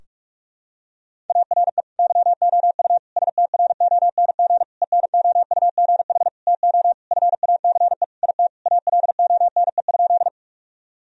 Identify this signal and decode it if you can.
Morse code, heard as a keyed tone, about 43 wpm, 700 Hz. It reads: ARE YOU STRONG ENOUGH TO FACE IT ALONE?
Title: Sentences from Top 1000 Words - Review Part 3 - 45wpm